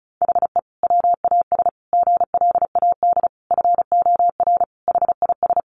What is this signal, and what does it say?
Morse code, heard as a keyed tone, about 34 wpm, 700 Hz. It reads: HE WAS GLAD FOR HIS
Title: Sentences from Top 1500 Words 35wpm